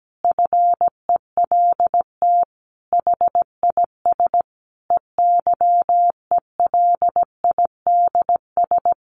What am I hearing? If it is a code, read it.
Morse code, heard as a keyed tone, about 17 wpm, 700 Hz. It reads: FELT HIS EYELIDS
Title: A Sound Of Thunder 17wpm